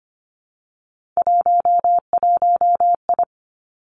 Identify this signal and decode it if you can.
Morse code, heard as a keyed tone, about 25 wpm, 700 Hz. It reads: 11I
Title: Three Challenging Letter-Number Combinations 25wpm